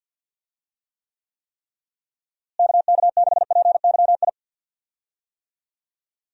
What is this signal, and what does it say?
Morse code, heard as a keyed tone, about 52 wpm, 700 Hz. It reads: KK6PYI
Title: USA Call Signs - 6 Characters - Repeated 3x - 50wpm